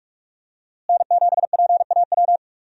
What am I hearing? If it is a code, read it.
Morse code, heard as a keyed tone, about 46 wpm, 700 Hz. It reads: N7PAW
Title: USA Call Signs - 5 Characters - Repeated 3x - 45wpm